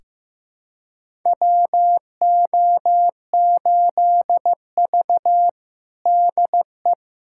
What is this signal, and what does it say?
Morse code, heard as a keyed tone, about 15 wpm, 700 Hz. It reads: WO8V DE
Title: USA Callsign DE Callsign - 4 Characters - 15wpm